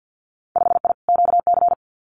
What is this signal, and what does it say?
Morse code, heard as a keyed tone, about 50 wpm, 700 Hz. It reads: 5I QC
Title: Winter Field Day Exchanges Encore 50wpm